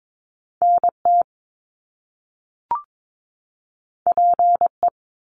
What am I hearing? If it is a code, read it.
Morse code, heard as a keyed tone, about 22 wpm, 700 Hz. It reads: NT PE
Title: Canadian Provinces and Territories 22wpm